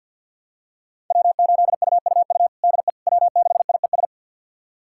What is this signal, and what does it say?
Morse code, heard as a keyed tone, about 48 wpm, 700 Hz. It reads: W8UUU DE W6IES
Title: USA Callsign DE Callsign - 5 Characters - 50wpm